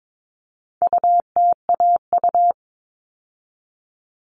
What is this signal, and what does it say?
Morse code, heard as a keyed tone, about 22 wpm, 700 Hz. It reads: UTAU
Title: All Cut Numbers - 4-Digit 22wpm